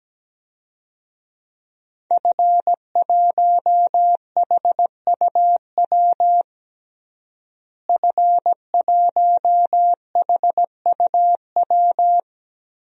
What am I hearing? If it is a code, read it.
Morse code, heard as a keyed tone, about 17 wpm, 700 Hz. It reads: F1HUW F1HUW
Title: DXCC-100 Call Signs - 5 Characters - Repeated 3x - 17wpm